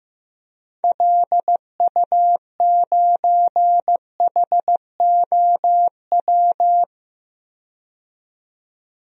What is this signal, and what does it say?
Morse code, heard as a keyed tone, about 15 wpm, 700 Hz. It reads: LU9HOW
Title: DXCC-100 Call Signs - 6 Characters - 15wpm